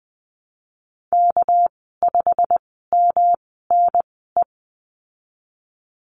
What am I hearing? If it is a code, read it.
Morse code, heard as a keyed tone, about 19 wpm, spaced as for 14 wpm, 700 Hz. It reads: K5MNE
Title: 238 Lesson 38g - Letter q - Call Signs 20wpm